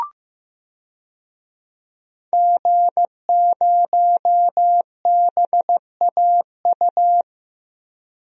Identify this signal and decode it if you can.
Morse code, heard as a keyed tone, about 15 wpm, 700 Hz. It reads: G0BAU
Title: DXCC-100 Call Signs - 5 Characters - Repeated 3x - 15wpm